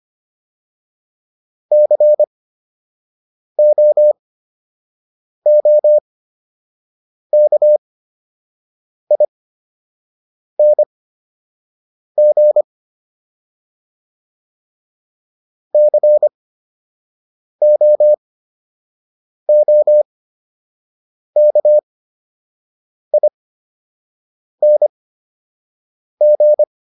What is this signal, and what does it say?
Morse code, heard as a keyed tone, about 25 wpm, spaced as for 6 wpm, 600 Hz. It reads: COOKING COOKING